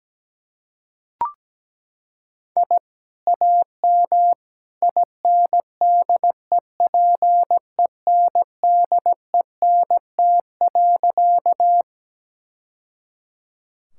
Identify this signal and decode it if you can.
Morse code, heard as a keyed tone, about 17 wpm, 700 Hz. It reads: I AM INDEPENDENT.
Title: Sentences from Top 1000 Words - Review Part 6 - 17wpm